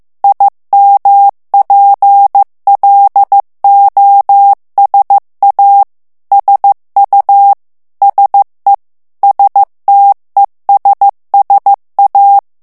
Morse code, heard as a keyed tone, about 15 wpm, 800 Hz. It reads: IMPLOSA SU SE STESSA
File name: ITALIEN texte clair 1 vitesse 14 mots minute